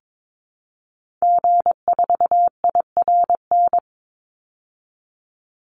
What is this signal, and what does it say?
Morse code, heard as a keyed tone, about 22 wpm, 700 Hz. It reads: G4IRN
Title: Call Signs Repeated 3x - 22wpm